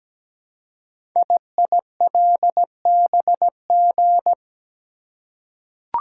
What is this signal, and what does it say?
Morse code, heard as a keyed tone, about 17 wpm, 700 Hz. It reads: IILBG
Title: Code Groups - 5 Letters - 17wpm